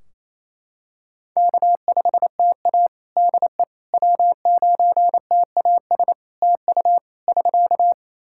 Morse code, heard as a keyed tone, about 27 wpm, 700 Hz. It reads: K5TA DE W9TAS TU <SK>
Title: Practice QSO - Exchange 4 - Ending - 28wpm